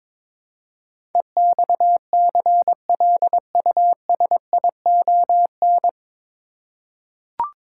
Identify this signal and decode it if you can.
Morse code, heard as a keyed tone, about 21 wpm, 700 Hz. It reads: EXCLUSION
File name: 9 Letter Words - First 3 Letters Given - 22wpm